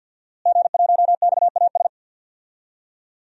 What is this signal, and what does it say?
Morse code, heard as a keyed tone, about 46 wpm, 700 Hz. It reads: G1XAS
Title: DXCC-100 Call Signs - 5 Characters - Repeated 3x - 50wpm